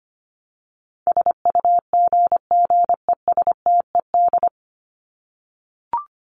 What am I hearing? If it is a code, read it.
Morse code, heard as a keyed tone, about 25 wpm, 700 Hz. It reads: SUGGESTED
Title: ED Suffix - Top 100 Words with Base Words 25wpm